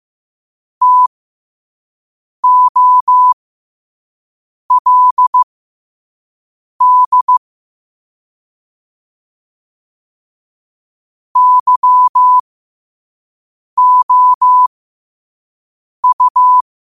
Morse code, heard as a keyed tone, about 15 wpm, spaced as for 5 wpm, 1000 Hz. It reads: TOLD YOU